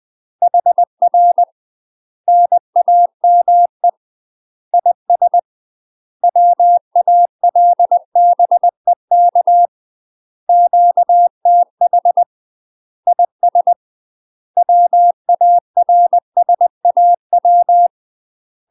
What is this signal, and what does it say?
Morse code, heard as a keyed tone, about 20 wpm, 700 Hz. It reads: HR NAME IS WALBEK QTH IS WARSAW
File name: UK MASTER QSOs TRAINING / 20WPM / UK MASTER QSOs 0003